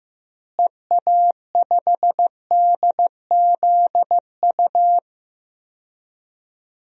Morse code, heard as a keyed tone, about 15 wpm, 700 Hz. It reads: EA5DZU
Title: DXCC-100 Call Signs - 6 Characters - Repeated 3x - 15wpm